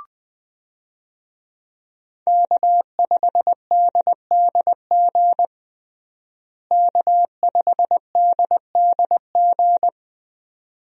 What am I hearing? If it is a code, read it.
Morse code, heard as a keyed tone, about 20 wpm, 700 Hz. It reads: K5DDG K5DDG
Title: USA Call Signs - 5 Characters - Repeated 2x - 20wpm